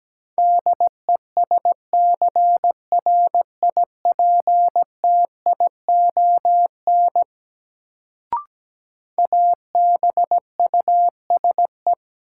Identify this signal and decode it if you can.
Morse code, heard as a keyed tone, about 17 wpm, 700 Hz. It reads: DESCRIPTION ABUSE
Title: Sentences from Top 1300 Words - New Word Drill - 17wpm